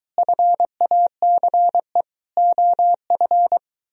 Morse code, heard as a keyed tone, about 22 wpm, 700 Hz. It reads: FACE OF